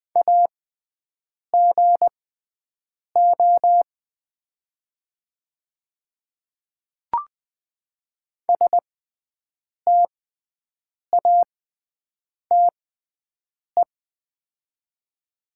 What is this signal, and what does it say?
Morse code, heard as a keyed tone, about 20 wpm, spaced as for 7 wpm, 700 Hz. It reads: AGO STATE
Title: 6x Character Spacing - Top 500 Words - 20wpm